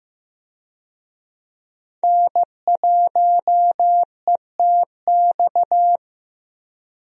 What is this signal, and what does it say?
Morse code, heard as a keyed tone, about 15 wpm, 700 Hz. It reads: N1ETX
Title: USA Call Signs - 5 Characters - Repeated 3x - 15wpm